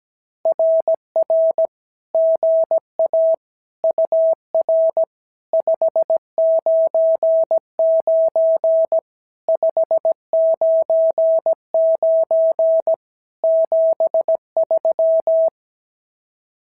Morse code, heard as a keyed tone, about 17 wpm, 650 Hz. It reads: RR GA UR 599 599 73
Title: SOTA QSO Practice 17wpm